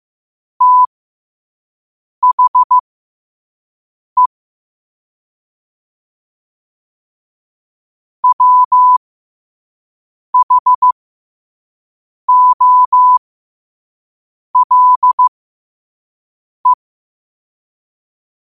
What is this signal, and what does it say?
Morse code, heard as a keyed tone, about 15 wpm, spaced as for 5 wpm, 1000 Hz. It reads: THE WHOLE
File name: qotdfiles / 5wpm